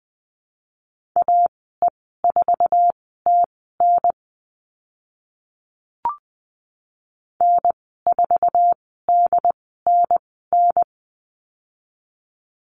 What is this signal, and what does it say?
Morse code, heard as a keyed tone, about 20 wpm, spaced as for 14 wpm, 700 Hz. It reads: AE4TN N4DNN
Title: 069 Lesson 12g - Letter d - Call Signs 20wpm